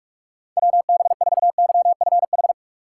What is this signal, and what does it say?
Morse code, heard as a keyed tone, about 46 wpm, 700 Hz. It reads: WB4YFH
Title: USA Call Signs - 6 Characters - 2x Character Spacing - 45wpm